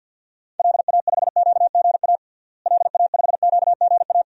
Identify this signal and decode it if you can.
Morse code, heard as a keyed tone, about 51 wpm, 700 Hz. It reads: LA5QGA LA5QGA
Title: DXCC-100 Call Signs - 6 Characters - Repeated 2x - 50wpm